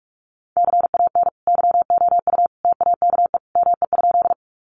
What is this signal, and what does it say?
Morse code, heard as a keyed tone, about 44 wpm, 700 Hz. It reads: CAN YOU TAKE ME?
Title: Sentences from Top 200 Words 45wpm